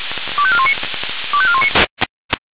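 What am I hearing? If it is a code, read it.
Five-tone selective call (ZVEI), heard as quick runs of five tones, tones of about 70 ms. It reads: 25419 25419
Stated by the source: Received on an SDR.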